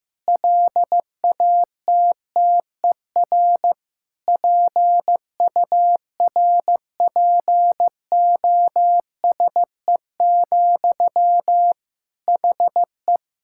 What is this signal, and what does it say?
Morse code, heard as a keyed tone, about 15 wpm, 700 Hz. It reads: LATTER PURPOSE, HE
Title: Sentences from Top 1600 Words 15wpm